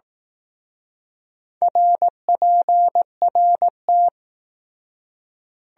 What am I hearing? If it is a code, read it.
Morse code, heard as a keyed tone, about 18 wpm, 700 Hz. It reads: RPRT
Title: Speed-Racing Abbreviations - 15wpm